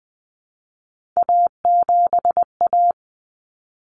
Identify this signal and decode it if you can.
Morse code, heard as a keyed tone, about 20 wpm, 700 Hz. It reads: A7A
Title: Common Cut Numbers - 3-Digit 20wpm